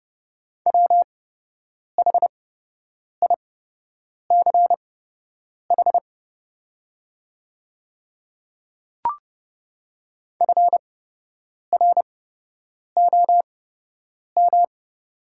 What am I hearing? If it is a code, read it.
Morse code, heard as a keyed tone, about 30 wpm, spaced as for 8 wpm, 700 Hz. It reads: WHICH FROM
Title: 8x Character Spacing - Top 100 Words - 30wpm